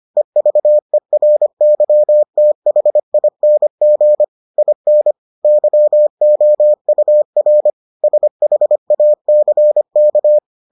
Morse code, heard as a keyed tone, about 25 wpm, 600 Hz. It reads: EVERYTHING IN YOUR SHACK